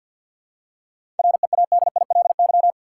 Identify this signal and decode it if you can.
Morse code, heard as a keyed tone, about 52 wpm, 700 Hz. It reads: READILY
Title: Seven Letter Words - 50wpm